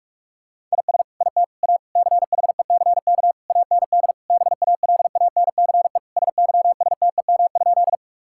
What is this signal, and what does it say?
Morse code, heard as a keyed tone, about 46 wpm, 700 Hz. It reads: IS IT A CHECK AND BALANCE SYSTEM?
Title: Sentences from Top 1000 Words - Review Part 6 - 45wpm